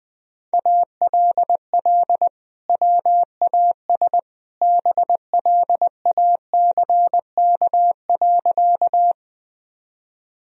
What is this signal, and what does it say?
Morse code, heard as a keyed tone, about 20 wpm, 700 Hz. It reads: ALL WAS BLACK.
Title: Sentences from Top 1000 Words - Review Part 2 - 20wpm